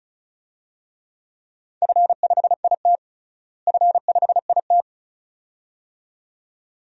Morse code, heard as a keyed tone, about 34 wpm, 700 Hz. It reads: F5IT F5IT
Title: DXCC-100 Call Signs - 4 Characters - Repeated 2x - 35wpm